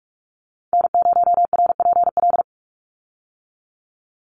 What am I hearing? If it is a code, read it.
Morse code, heard as a keyed tone, about 44 wpm, 700 Hz. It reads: N0RPL
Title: USA Call Signs 45wpm